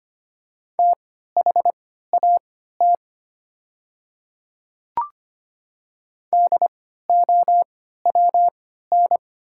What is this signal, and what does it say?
Morse code, heard as a keyed tone, about 25 wpm, spaced as for 14 wpm, 700 Hz. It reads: THAT DOWN
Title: 3x Character Spacing - Top 100 Words - 25wpm